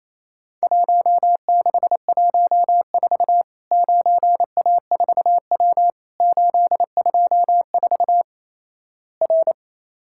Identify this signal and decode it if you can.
Morse code, heard as a keyed tone, about 28 wpm, 700 Hz. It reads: 1614 9A4W 824 R
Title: Worked All Europe DX Contest - QTC Exchange - 28wpm